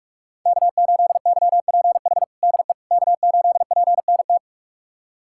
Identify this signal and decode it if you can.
Morse code, heard as a keyed tone, about 46 wpm, 700 Hz. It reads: K8YPH DE K8PNT
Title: USA Callsign DE Callsign - 5 Characters - 45wpm